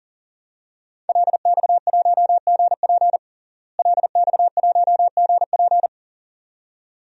Morse code, heard as a keyed tone, about 40 wpm, 700 Hz. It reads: LX1GP LX1GP
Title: DXCC-100 Call Signs - 5 Characters - Repeated 2x - 40wpm